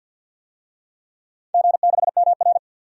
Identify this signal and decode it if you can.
Morse code, heard as a keyed tone, about 51 wpm, 700 Hz. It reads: G6MR
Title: DXCC-100 Call Signs - 4 Characters - 50wpm